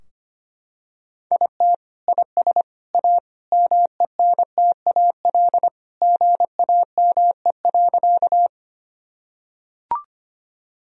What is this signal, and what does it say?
Morse code, heard as a keyed tone, about 25 wpm, 700 Hz. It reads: IT IS A MENTAL GAME.
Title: Sentences from Top 1000 Words - Review Part 4 - 25wpm